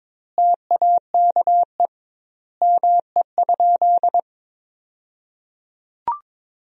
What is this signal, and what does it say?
Morse code, heard as a keyed tone, about 21 wpm, 700 Hz. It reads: TAKE ME?
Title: 2x Word-Spacing - Sentences from Top 200 Words - 22wpm